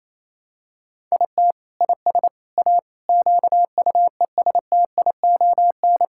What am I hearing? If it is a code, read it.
Morse code, heard as a keyed tone, about 28 wpm, 700 Hz. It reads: IT IS A QUESTION
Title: Sentences from Top 1000 Words - Review Part 6 - 28wpm